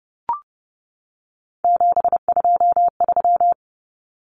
Morse code, heard as a keyed tone, about 30 wpm, 700 Hz. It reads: 723
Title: Common Cut Numbers - 3-Digit 30wpm